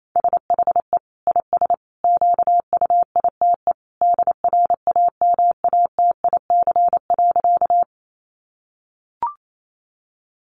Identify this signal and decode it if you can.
Morse code, heard as a keyed tone, about 28 wpm, 700 Hz. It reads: SHE IS QUITE DRAMATIC.
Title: Sentences from Top 1300 Words 28wpm